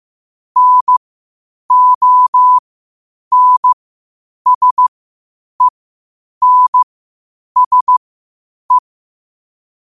Morse code, heard as a keyed tone, about 15 wpm, spaced as for 8 wpm, 1000 Hz. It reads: NONSENSE
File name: 8wpm